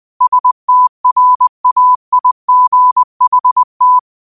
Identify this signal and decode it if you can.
Morse code, heard as a keyed tone, about 20 wpm, 1000 Hz. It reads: STRAIGHT
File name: 20wpm